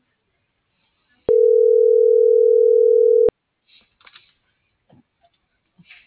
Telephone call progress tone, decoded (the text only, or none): ringback tone